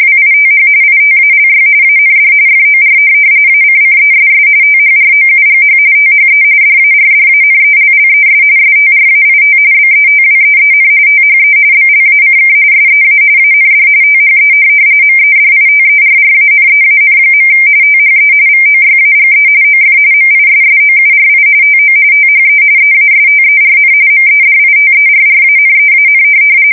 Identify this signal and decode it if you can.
RTTY, about 45.45 baud. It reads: WITH ITALY AND FOR ACCESS THROUGH THE DARDANELLES STRAIT BETWEEN THE BLACK SEA AND THE MEDITERRANEAN. THE VOYAGE OF MISSOURI TO THE EASTERN MEDITERRANEAN
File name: fsk170 / USS-Missouri